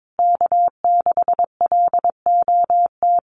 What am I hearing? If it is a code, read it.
Morse code, heard as a keyed tone, about 22 wpm, 700 Hz. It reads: K6LOT
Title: USA Call Signs Repeated 2x 22wpm